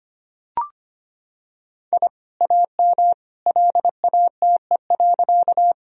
Morse code, heard as a keyed tone, about 25 wpm, 700 Hz. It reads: I AM LATE.
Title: Sentences from Top 1000 Words - Review Part 2 - 25wpm